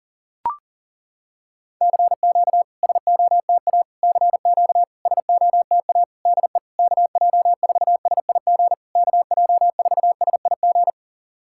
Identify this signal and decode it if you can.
Morse code, heard as a keyed tone, about 40 wpm, 700 Hz. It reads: CQ SOTA CQ SOTA DE KJ4SIG KJ4SIG
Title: SOTA QSO Practice 40wpm